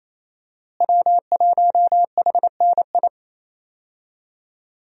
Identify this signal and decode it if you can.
Morse code, heard as a keyed tone, about 27 wpm, 700 Hz. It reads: W1HNI
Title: USA Call Signs - 5 Characters - Repeated 2x - 28wpm